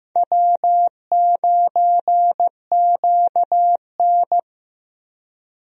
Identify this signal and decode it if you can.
Morse code, heard as a keyed tone, about 15 wpm, 700 Hz. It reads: W9QN
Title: USA Call Signs - 4 Characters - Repeated 2x - 15wpm